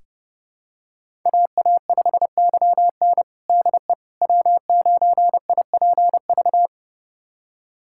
Morse code, heard as a keyed tone, about 31 wpm, 700 Hz. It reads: AA5YN DE W9IPV
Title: USA Callsign DE Callsign - 5 Characters - 30wpm